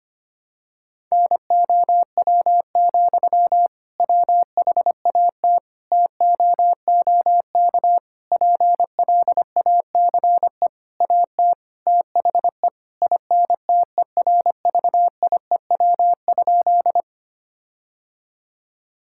Morse code, heard as a keyed tone, about 25 wpm, 700 Hz. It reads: NOW, WHAT TOOK PLACE AT THE INTERVIEW?
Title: Sentences from Top 1600 Words 25wpm